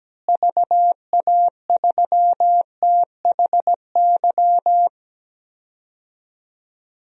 Morse code, heard as a keyed tone, about 17 wpm, 700 Hz. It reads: VA3THY
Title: DXCC-100 Call Signs - 6 Characters - Repeated 3x - 17wpm